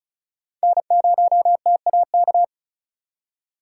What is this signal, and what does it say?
Morse code, heard as a keyed tone, about 44 wpm, 700 Hz. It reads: N0TAK
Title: Speed Racing - USA 1x3 Call Signs 35wpm